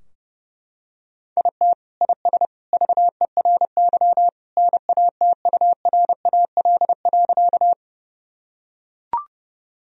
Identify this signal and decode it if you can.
Morse code, heard as a keyed tone, about 28 wpm, 700 Hz. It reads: IT IS VERY NATURAL.
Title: Sentences from Top 1000 Words - Review Part 4 - 30wpm